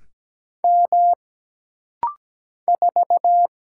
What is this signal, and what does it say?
Morse code, heard as a keyed tone, about 17 wpm, 700 Hz. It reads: M 4
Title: Single Letter-Number - Rapid-Fire 17wpm